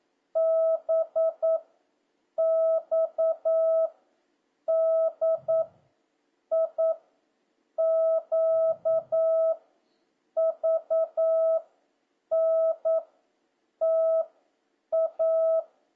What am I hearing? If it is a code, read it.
Morse code, heard as a keyed tone, about 9 wpm, 650 Hz. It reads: B X D I Q V N T A